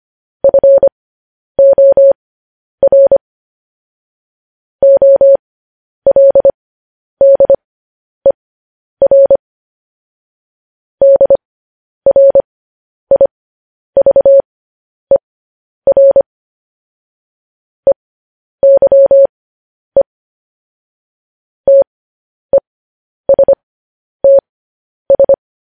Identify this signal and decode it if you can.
Morse code, heard as a keyed tone, about 25 wpm, spaced as for 10 wpm, 550 Hz. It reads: FOR OLDER DRIVER EYE TESTS
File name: podcast / News headlines-1x-25x10